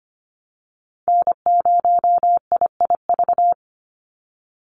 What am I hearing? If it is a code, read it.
Morse code, heard as a keyed tone, about 25 wpm, 700 Hz. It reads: N0IIV
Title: USA Call Signs 25wpm